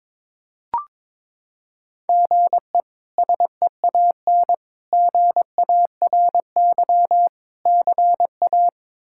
Morse code, heard as a keyed tone, about 22 wpm, 700 Hz. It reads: GE SEAN GARY CA
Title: K1USN SST QSO Practice - 22wpm